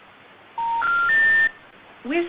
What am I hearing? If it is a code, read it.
Special information tone, with speech.
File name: recordings / RE6b716eabbcaed fb0e61dc25c9